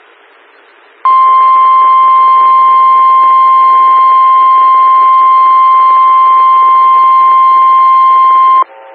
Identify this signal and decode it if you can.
Emergency Alert System (SAME) tones (the 1050 Hz attention tone). It is heard from 1.0 to 8.7 s.
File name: WX9MCS 147255 MHz SKYWARN Repeater